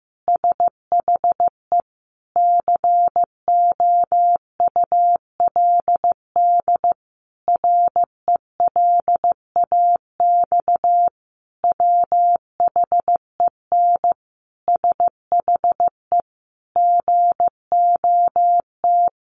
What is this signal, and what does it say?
Morse code, heard as a keyed tone, about 15 wpm, 700 Hz. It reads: SHE COULD RELAX WHEN SHE GOT
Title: Sentences from Top 1400 Words 15wpm